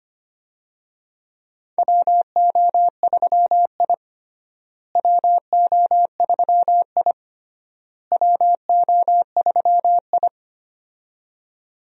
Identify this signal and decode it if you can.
Morse code, heard as a keyed tone, about 25 wpm, 700 Hz. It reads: WO3I WO3I WO3I
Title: USA Call Signs - 4 Characters - Repeated 3x - 25wpm